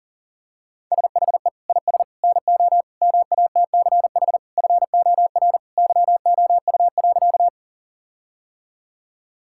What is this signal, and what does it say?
Morse code, heard as a keyed tone, about 40 wpm, 700 Hz. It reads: SHE IS NO MATCH FOR YOU.
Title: Sentences from Top 1600 Words 40wpm